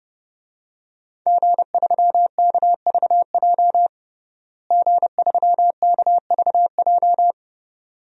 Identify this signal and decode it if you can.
Morse code, heard as a keyed tone, about 30 wpm, 700 Hz. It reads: G3KVJ G3KVJ
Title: DXCC-100 Call Signs - 5 Characters - Repeated 2x - 30wpm